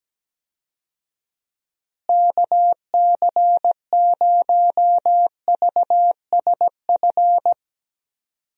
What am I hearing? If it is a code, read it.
Morse code, heard as a keyed tone, about 17 wpm, 700 Hz. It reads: KC0VSF
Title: USA Call Signs - 6 Characters - Repeated 2x - 17wpm